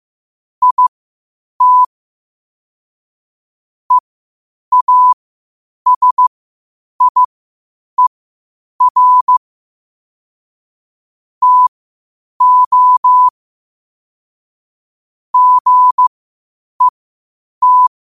Morse code, heard as a keyed tone, about 15 wpm, spaced as for 8 wpm, 1000 Hz. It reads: IT EASIER TO GET
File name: qotdfiles / 8wpm